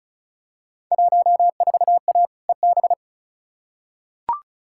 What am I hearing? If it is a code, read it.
Morse code, heard as a keyed tone, about 34 wpm, 700 Hz. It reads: 14A EB
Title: Field Day Exchanges Encore - 35wpm